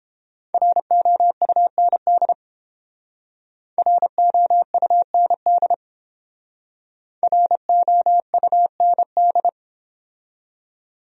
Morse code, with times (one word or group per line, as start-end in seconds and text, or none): 0.5-2.3 ROUND
3.8-5.7 ROUND
7.2-9.5 ROUND